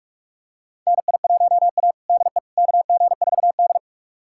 Morse code, heard as a keyed tone, about 45 wpm, 700 Hz. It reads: NI1A DE KG4D
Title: USA Callsign DE Callsign - 4 Characters - 45wpm